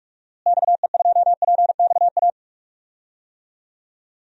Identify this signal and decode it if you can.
Morse code, heard as a keyed tone, about 46 wpm, 700 Hz. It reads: XE2PXA
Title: DXCC-100 Call Signs - 6 Characters - Repeated 3x - 45wpm